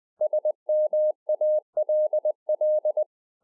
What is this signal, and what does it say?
Morse code, heard as a keyed tone, about 20 wpm, 610 Hz. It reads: SMALL
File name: 2011-04-mp3 / A SMALL TRANCEIVER 3 20WPM